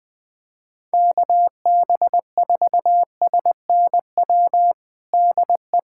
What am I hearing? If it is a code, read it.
Morse code, heard as a keyed tone, about 20 wpm, 700 Hz. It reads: KB4SNW DE
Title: USA Callsign DE Callsign 20wpm